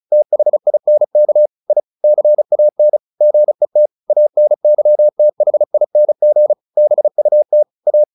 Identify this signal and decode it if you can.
Morse code, heard as a keyed tone, about 34 wpm, 600 Hz. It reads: THINK I CAN GET ANYTHING BUT A
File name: the old man and the sea / the old man and the sea 1 35wpm